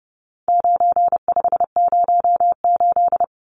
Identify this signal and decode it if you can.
Morse code, heard as a keyed tone, about 30 wpm, 700 Hz. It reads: 9508
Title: Numbers - 4-Digit 30wpm